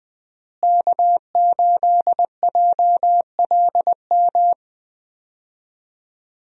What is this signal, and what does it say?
Morse code, heard as a keyed tone, about 20 wpm, 700 Hz. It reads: K8JLM
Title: USA Call Signs - 5 Characters - Repeated 2x - 20wpm